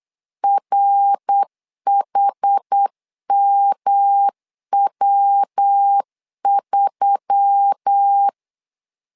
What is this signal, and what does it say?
Morse code, heard as a keyed tone, about 8 wpm, 800 Hz. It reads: RHMW3
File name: LEZIONI / 4 Q1S3 8wpm 3037